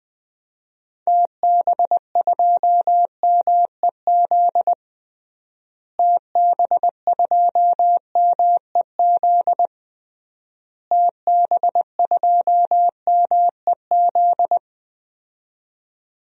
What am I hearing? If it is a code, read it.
Morse code, heard as a keyed tone, about 21 wpm, 700 Hz. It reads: TB2MEZ TB2MEZ TB2MEZ
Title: DXCC-100 Call Signs - 6 Characters - Repeated 3x - 20wpm